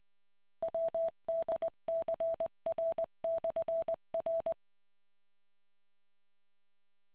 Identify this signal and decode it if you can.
Morse code, heard as a keyed tone, about 20 wpm, 660 Hz. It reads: WDCR/R